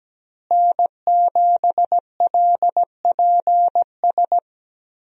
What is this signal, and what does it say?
Morse code, heard as a keyed tone, about 17 wpm, 700 Hz. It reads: N7LPS
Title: USA Call Signs - 5 Characters - Repeated 3x - 17wpm